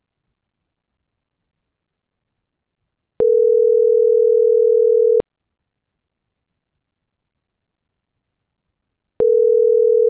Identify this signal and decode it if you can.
Ringback tone (North America), 440 + 480 Hz.